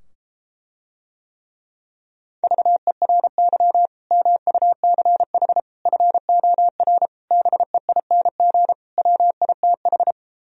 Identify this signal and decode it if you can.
Morse code, heard as a keyed tone, about 33 wpm, 700 Hz. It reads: VERY MUCH FOR BEING WITH